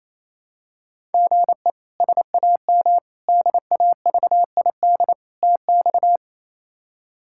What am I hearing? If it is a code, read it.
Morse code, heard as a keyed tone, about 28 wpm, 700 Hz. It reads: GE SAM DAVID TX
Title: K1USN SST QSO Practice - 28wpm